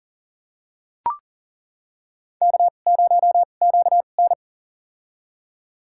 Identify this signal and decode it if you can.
Morse code, heard as a keyed tone, about 40 wpm, spaced as for 29 wpm, 700 Hz. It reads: K0QN
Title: USA Call Signs - 4 Characters - 2x Character Spacing - 40wpm